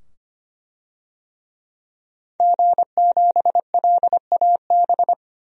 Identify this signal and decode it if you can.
Morse code, heard as a keyed tone, about 25 wpm, 700 Hz. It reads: G7LAB
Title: DXCC-100 Call Signs - 5 Characters - 25wpm